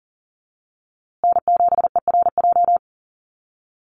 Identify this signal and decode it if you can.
Morse code, heard as a keyed tone, about 40 wpm, 700 Hz. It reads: N7ERJ
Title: USA Call Signs Repeated 2x 40wpm